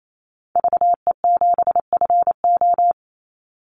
Morse code, heard as a keyed tone, about 28 wpm, 700 Hz. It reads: VE7FO
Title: Call Signs Repeated 3x - 28wpm